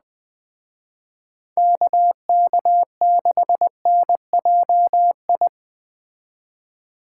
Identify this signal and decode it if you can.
Morse code, heard as a keyed tone, about 20 wpm, 700 Hz. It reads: KK6NJI
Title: Speed Racing - USA 2x3 Call Signs 17wpm